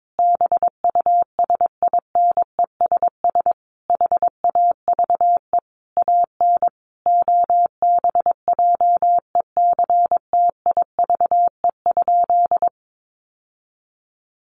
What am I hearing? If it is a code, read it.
Morse code, heard as a keyed tone, about 22 wpm, 700 Hz. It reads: BUSINESS HAVE AN OBJECTIVE?
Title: Sentences from Top 1200 Words 22wpm